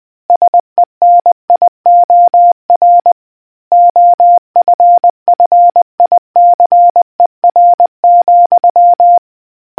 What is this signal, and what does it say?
Morse code, heard as a keyed tone, about 20 wpm, 700 Hz. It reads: SENIOR OFFICER,